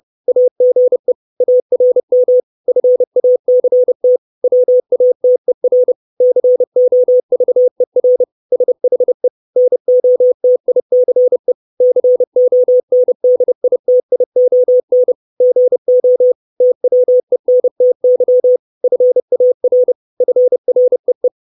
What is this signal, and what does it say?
Morse code, heard as a keyed tone, about 30 wpm, 500 Hz. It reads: AGE ARM FACT WATER COVER SHE NOTICE CONDITION GO TWENTY FAR FREE
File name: 632 Words 30WPM